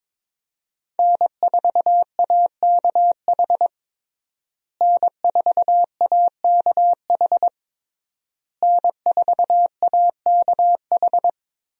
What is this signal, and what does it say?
Morse code, heard as a keyed tone, about 21 wpm, 700 Hz. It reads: N4AKH N4AKH N4AKH
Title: USA Call Signs - 5 Characters - Repeated 3x - 22wpm